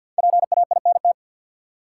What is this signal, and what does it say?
Morse code, heard as a keyed tone, about 53 wpm, 700 Hz. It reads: PAINT
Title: Sentences from Top 1000 Words - Review Part 6 - New Word Drill - 50wpm